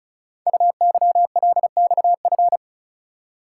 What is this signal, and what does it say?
Morse code, heard as a keyed tone, about 36 wpm, 700 Hz. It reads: UYLXF
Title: Code Groups - 5 Letters - 35wpm